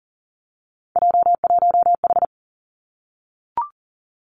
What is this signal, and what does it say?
Morse code, heard as a keyed tone, about 40 wpm, 700 Hz. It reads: J1H
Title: Three Challenging Letter-Number Combinations 40wpm